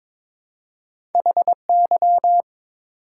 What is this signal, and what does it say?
Morse code, heard as a keyed tone, about 22 wpm, 700 Hz. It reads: HY
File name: practice / 22wpm / Two Letter-Number Combinations - 22wpm